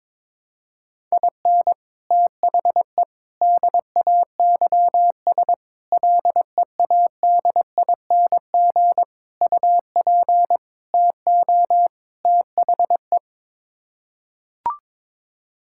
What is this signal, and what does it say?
Morse code, heard as a keyed tone, about 22 wpm, 700 Hz. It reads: IN THE DAYS LEADING UP TO THE
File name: practice / 22wpm / Sets of 7 Words - 22wpm